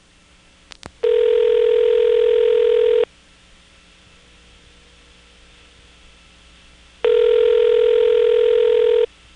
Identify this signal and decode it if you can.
Ringback tone (North America), 440 + 480 Hz.
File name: phonetrips / Nantes1HQ